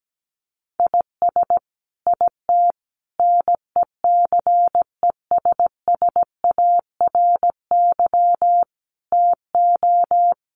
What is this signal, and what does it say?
Morse code, heard as a keyed tone, about 17 wpm, 700 Hz. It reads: IS IT NECESSARY TO
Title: Sentences from Top 1500 Words 17wpm